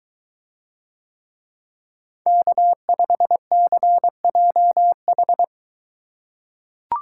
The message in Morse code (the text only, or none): K5CJH